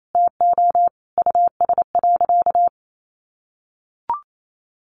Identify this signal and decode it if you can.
Morse code, heard as a keyed tone, about 28 wpm, 700 Hz. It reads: TO US.
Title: Sentences from Top 600 Words 28wpm